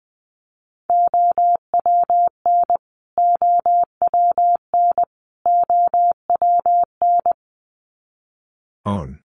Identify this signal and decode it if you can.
Morse code, heard as a keyed tone, about 20 wpm, 700 Hz. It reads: OWN OWN OWN
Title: Top 500 Words Repeated 20wpm